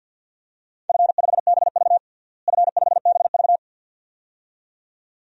Morse code, heard as a keyed tone, about 47 wpm, 700 Hz. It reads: F5BV F5BV
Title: DXCC-100 Call Signs - 4 Characters - Repeated 2x - 50wpm